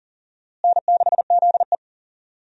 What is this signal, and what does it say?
Morse code, heard as a keyed tone, about 40 wpm, 700 Hz. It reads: N6ZE
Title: USA Call Signs - 4 Characters - Repeated 3x - 40wpm